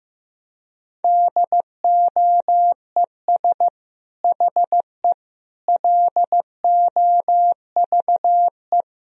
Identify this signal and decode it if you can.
Morse code, heard as a keyed tone, about 15 wpm, 700 Hz. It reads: DOES HE LOVE
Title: Sentences from Top 400 Words and Abbreviations 15wpm